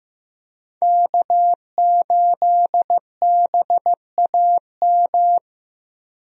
Morse code, heard as a keyed tone, about 15 wpm, 700 Hz. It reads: K8BAM
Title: USA Call Signs - 5 Characters - Repeated 2x - 15wpm